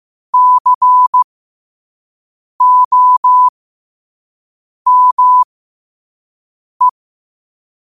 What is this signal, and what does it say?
Morse code, heard as a keyed tone, about 15 wpm, spaced as for 5 wpm, 1000 Hz. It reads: COME
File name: qotdfiles / 5wpm